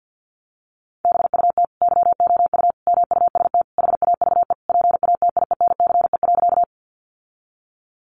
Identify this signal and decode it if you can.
Morse code, heard as a keyed tone, about 50 wpm, 700 Hz. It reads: BUT YOU MUST HAVE PATIENCE.
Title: Sentences from Top 1300 Words 50wpm